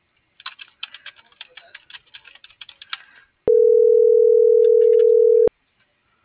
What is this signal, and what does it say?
Ringback tone (North America), 440 + 480 Hz.